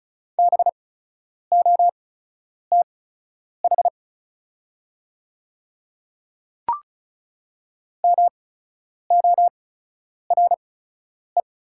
Morse code, heard as a keyed tone, about 36 wpm, spaced as for 10 wpm, 700 Hz. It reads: BOTH MORE
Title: 8x Character Spacing - Top 300 Words - 35wpm